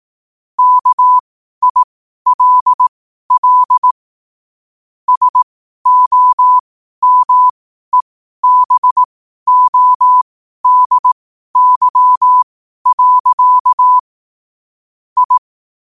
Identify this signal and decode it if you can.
Morse code, heard as a keyed tone, about 18 wpm, spaced as for 12 wpm, 1000 Hz. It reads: KILL SOMEBODY. I
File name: qotdfiles / 12wpm